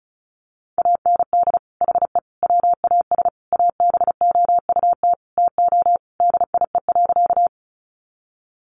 Morse code, heard as a keyed tone, about 34 wpm, 700 Hz. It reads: AND HE WAS ABOUT TO DIE.
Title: Sentences from Top 1300 Words 35wpm